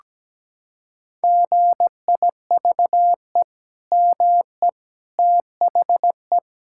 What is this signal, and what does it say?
Morse code, heard as a keyed tone, about 17 wpm, 700 Hz. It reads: GIVE ME THE
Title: Sentences from Top 1600 Words 17wpm